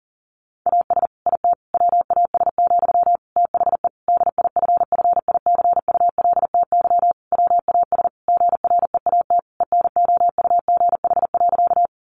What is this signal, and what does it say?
Morse code, heard as a keyed tone, about 40 wpm, 700 Hz. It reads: AS IT WAS, THE DIFFICULTY WAS GREAT ENOUGH.
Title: Sentences from Top 1500 Words 40wpm